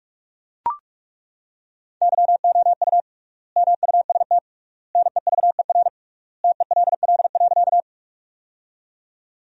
Morse code, heard as a keyed tone, about 41 wpm, 700 Hz. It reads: YOU MUST NEVER TELL.
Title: 3x Word-Spacing - Sentences from Top 300 Words - 45wpm